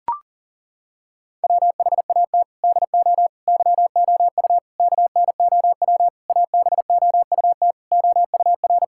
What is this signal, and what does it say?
Morse code, heard as a keyed tone, about 40 wpm, 700 Hz. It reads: WHAT DO YOU KNOW ABOUT OUR
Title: Sentences from Top 1000 Words - Review Part 6 - 40wpm